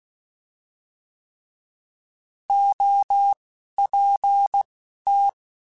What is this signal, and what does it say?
Morse code, heard as a keyed tone, about 16 wpm, spaced as for 11 wpm, 800 Hz. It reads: OPT
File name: Alphabetical / Alphabetical mp3s / Os @10